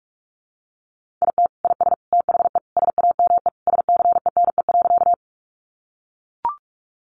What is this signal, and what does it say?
Morse code, heard as a keyed tone, about 44 wpm, 700 Hz. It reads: IT IS THE SAME SCENE.
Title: Sentences from Top 1100 Words 45wpm